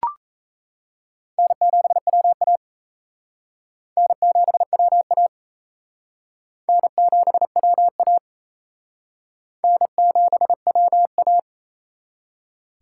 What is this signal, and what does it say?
Morse code, heard as a keyed tone, about 36 wpm, 700 Hz. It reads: N7WA N7WA N7WA N7WA